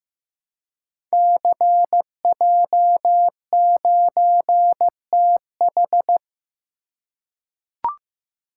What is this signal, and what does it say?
Morse code, heard as a keyed tone, about 15 wpm, 700 Hz. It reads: CJ9TH
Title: Code Groups - 5 Letter:Numbers - 15wpm